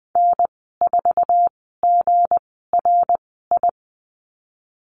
Morse code, heard as a keyed tone, about 20 wpm, spaced as for 15 wpm, 700 Hz. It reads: N4GRI
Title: 179 Lesson 29g - Slash - Call Signs 20wpm